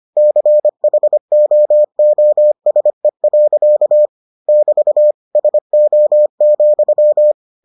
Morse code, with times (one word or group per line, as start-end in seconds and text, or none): 0.2-4.1 CHOOSE.
4.5-5.1 <BT>
5.4-7.3 SO,